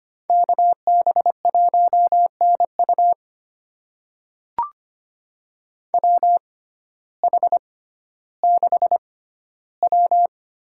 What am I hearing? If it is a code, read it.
Morse code, heard as a keyed tone, about 25 wpm, spaced as for 14 wpm, 700 Hz. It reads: KB1NU WH6W
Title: USA Call Signs - 5 Characters - 6x Character Spacing - 25wpm